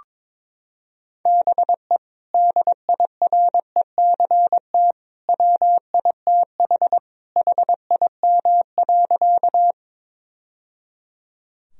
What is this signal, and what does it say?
Morse code, heard as a keyed tone, about 21 wpm, 700 Hz. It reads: BE DIRECT WITH HIM.
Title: Sentences from Top 1000 Words - Review Part 2 - 22wpm